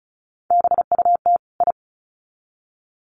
Morse code, heard as a keyed tone, about 34 wpm, 700 Hz. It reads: BUT I
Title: Sets of 2 Words 35wpm